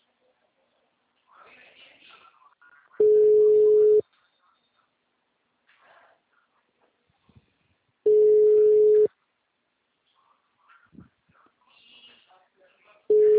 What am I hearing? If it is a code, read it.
Ringback tone (Europe, 425 Hz), with speech.